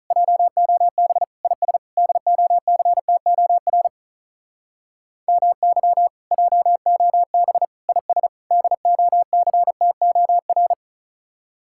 Morse code, heard as a keyed tone, about 44 wpm, 700 Hz. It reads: JOB IS DOCTOR MY JOB IS DOCTOR